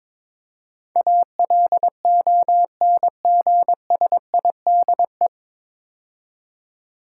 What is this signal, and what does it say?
Morse code, heard as a keyed tone, about 22 wpm, 700 Hz. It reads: ALONGSIDE
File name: 9 Letter Words - First 3 Letters Given - 22wpm